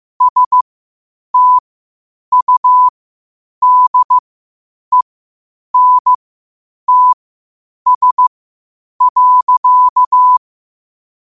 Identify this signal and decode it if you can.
Morse code, heard as a keyed tone, about 15 wpm, spaced as for 8 wpm, 1000 Hz. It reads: STUDENTS.
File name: qotdfiles / 8wpm